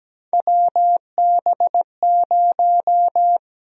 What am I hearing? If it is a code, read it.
Morse code, heard as a keyed tone, about 17 wpm, 700 Hz. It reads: WB0
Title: USA Call Sign Prefixes - Repeated 2x - 17wpm